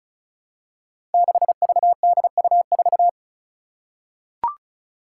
Morse code, heard as a keyed tone, about 36 wpm, 700 Hz. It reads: 6VDU4
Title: Five Challenging Letter-Number Combinations 35wpm